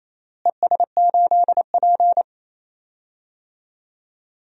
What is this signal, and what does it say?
Morse code, heard as a keyed tone, about 27 wpm, 700 Hz. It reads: ES8P
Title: DXCC-100 Call Signs - 4 Characters - Repeated 2x - 28wpm